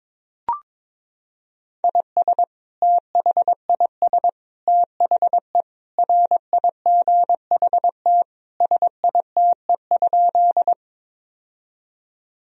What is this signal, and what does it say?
Morse code, heard as a keyed tone, about 22 wpm, 700 Hz. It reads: IS THIS THE RIGHT SITE?
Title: Sentences from Top 1000 Words - Review Part 4 - 22wpm